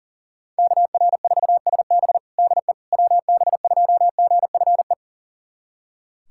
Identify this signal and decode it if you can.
Morse code, heard as a keyed tone, about 40 wpm, 700 Hz. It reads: KR4SB DE WB2GFE
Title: USA Callsign DE Callsign 40wpm